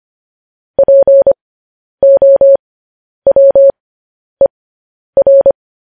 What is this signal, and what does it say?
Morse code, heard as a keyed tone, about 25 wpm, spaced as for 10 wpm, 550 Hz. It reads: POWER
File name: podcast / News headlines-1x-25x10